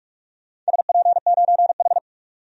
Morse code, heard as a keyed tone, about 47 wpm, 700 Hz. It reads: SP9H
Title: DXCC-100 Call Signs - 4 Characters - 45wpm